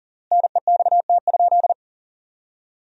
Morse code, heard as a keyed tone, about 37 wpm, 700 Hz. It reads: NEXT?
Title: 3x Word-Spacing - Sentences from Top 300 Words - 40wpm